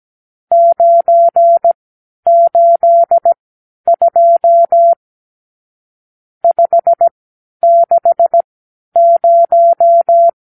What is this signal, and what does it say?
Morse code, heard as a keyed tone, about 17 wpm, spaced as for 10 wpm, 680 Hz. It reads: 982 560
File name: arrl / Review (1-0)